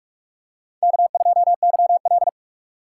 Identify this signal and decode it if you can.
Morse code, heard as a keyed tone, about 46 wpm, 700 Hz. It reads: K2YL
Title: USA Call Signs - 2x Character Spacing - 45wpm